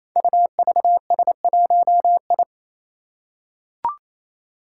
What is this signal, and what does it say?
Morse code, heard as a keyed tone, about 27 wpm, 700 Hz. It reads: UVS1I
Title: Five Challenging Letter-Number Combinations 28wpm